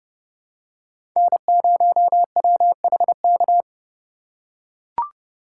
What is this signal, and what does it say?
Morse code, heard as a keyed tone, about 29 wpm, 700 Hz. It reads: N0WHK